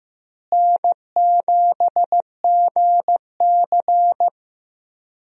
Morse code, heard as a keyed tone, about 15 wpm, 700 Hz. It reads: N7GC
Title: USA Call Signs - 4 Characters - Repeated 2x - 15wpm